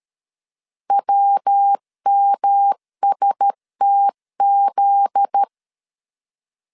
Morse code, heard as a keyed tone, about 13 wpm, 800 Hz. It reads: WMSTZ
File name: LEZIONI / 10 SIGNS / 10B INT 12wpm 2414